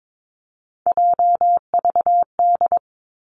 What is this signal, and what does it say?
Morse code, heard as a keyed tone, about 22 wpm, 700 Hz. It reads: JVD
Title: Three Challenging Letter-Number Combinations 22wpm